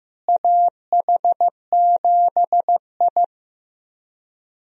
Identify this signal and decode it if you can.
Morse code, heard as a keyed tone, about 15 wpm, 700 Hz. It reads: AH7I
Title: USA Call Signs - 4 Characters - Repeated 3x - 15wpm